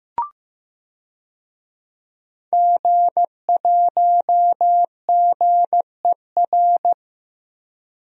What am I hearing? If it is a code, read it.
Morse code, heard as a keyed tone, about 15 wpm, 700 Hz. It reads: G1GER
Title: DXCC-100 Call Signs - 5 Characters - Repeated 3x - 15wpm